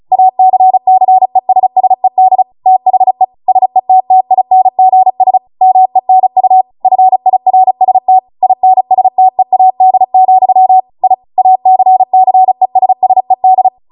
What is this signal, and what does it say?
Morse code, heard as a keyed tone, about 36 wpm, 750 Hz. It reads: ACCESSED THE SETTINGS MENU FIRST INSTEAD, I ACCESSED